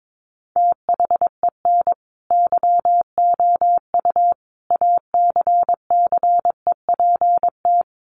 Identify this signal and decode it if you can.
Morse code, heard as a keyed tone, about 22 wpm, 700 Hz. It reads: THEN YOU ACCEPT
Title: Sentences from Top 1300 Words 22wpm